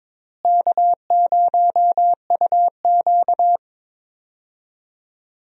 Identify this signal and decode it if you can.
Morse code, heard as a keyed tone, about 22 wpm, 700 Hz. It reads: K0UQ
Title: USA Call Signs - 4 Characters - Repeated 3x - 22wpm